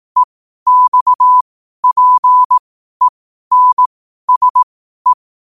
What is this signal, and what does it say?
Morse code, heard as a keyed tone, about 18 wpm, spaced as for 12 wpm, 1000 Hz. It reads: EXPENSE
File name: qotdfiles / 12wpm